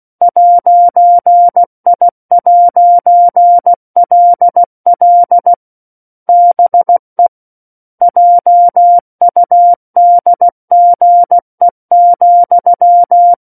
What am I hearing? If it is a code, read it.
Morse code, heard as a keyed tone, about 16 wpm, 700 Hz. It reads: 'I'LL BE JUDGE,
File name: 03-Chapter-03 16 WPM